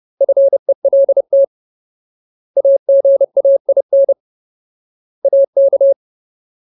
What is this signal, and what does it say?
Morse code, heard as a keyed tone, about 30 wpm, 550 Hz. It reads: FELT AGAIN AK
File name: Ultimate Word List / 30WPM Ultimate